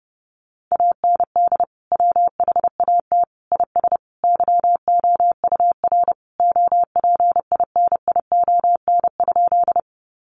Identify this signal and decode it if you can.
Morse code, heard as a keyed tone, about 30 wpm, 700 Hz. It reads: AND WHAT IS YOUR OPINION?
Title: Sentences from Top 1200 Words 30wpm